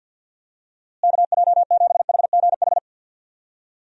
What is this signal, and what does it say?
Morse code, heard as a keyed tone, about 45 wpm, 700 Hz. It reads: KJ7HGH
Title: USA Call Signs - 6 Characters - Repeated 2x - 50wpm